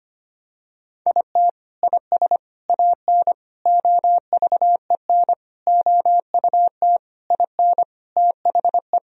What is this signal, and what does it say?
Morse code, heard as a keyed tone, about 25 wpm, 700 Hz. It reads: IT IS AN OVEN OUT IN THE
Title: Sentences from Top 1000 Words - Review Part 4 - 25wpm